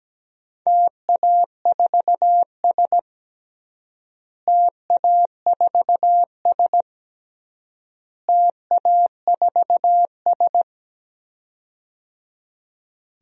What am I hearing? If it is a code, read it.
Morse code, heard as a keyed tone, about 17 wpm, 700 Hz. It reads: TA4S TA4S TA4S
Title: DXCC-100 Call Signs - 4 Characters - Repeated 3x - 17wpm